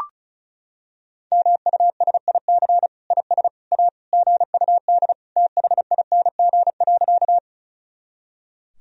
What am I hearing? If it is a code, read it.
Morse code, heard as a keyed tone, about 35 wpm, 700 Hz. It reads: MUSIC IS A GUD THING.
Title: Sentences from Top 300 Words and Abbreviations 35wpm